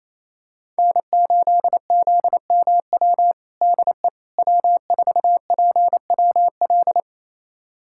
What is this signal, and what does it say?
Morse code, heard as a keyed tone, about 28 wpm, 700 Hz. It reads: N8ZMW DE W4PWL
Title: USA Callsign DE Callsign - 5 Characters - 28wpm